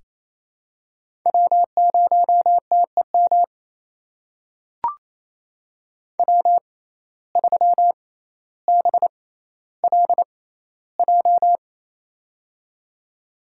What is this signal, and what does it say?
Morse code, heard as a keyed tone, about 28 wpm, spaced as for 14 wpm, 700 Hz. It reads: W0TEM W3BLJ
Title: USA Call Signs - 5 Characters - 6x Character Spacing - 28wpm